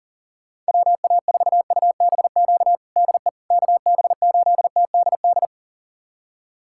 Morse code, heard as a keyed tone, about 40 wpm, 700 Hz. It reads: WA4UBQ DE KB8TDD
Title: USA Callsign DE Callsign - 6 Characters - 40wpm